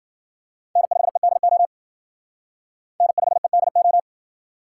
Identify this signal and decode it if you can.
Morse code, heard as a keyed tone, about 52 wpm, 700 Hz. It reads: N5EDO N5EDO